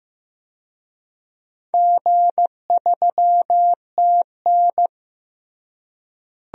Morse code, heard as a keyed tone, about 16 wpm, 700 Hz. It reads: G3TN
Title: DXCC-100 Call Signs - 4 Characters - Repeated 3x - 15wpm